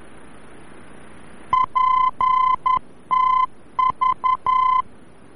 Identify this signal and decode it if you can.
Morse code, heard as a keyed tone, about 11 wpm, 1020 Hz. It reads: PTV
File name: france / mx / PTV-VOR-F